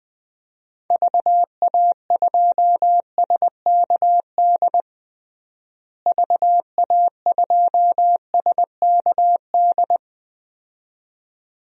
Morse code, heard as a keyed tone, about 20 wpm, 700 Hz. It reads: VA2SKD VA2SKD
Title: DXCC-100 Call Signs - 6 Characters - Repeated 2x - 20wpm